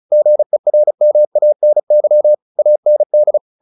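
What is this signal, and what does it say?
Morse code, heard as a keyed tone, about 34 wpm, 600 Hz. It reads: GERMANY AND
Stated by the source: Morse practice at 35 wpm from CW med Gnister, sending Ken Follett's novel Never in English.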